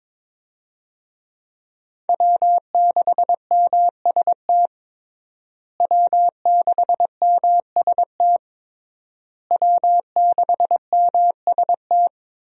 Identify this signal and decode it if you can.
Morse code, heard as a keyed tone, about 22 wpm, 700 Hz. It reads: W6MST W6MST W6MST
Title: USA Call Signs - 5 Characters - Repeated 3x - 22wpm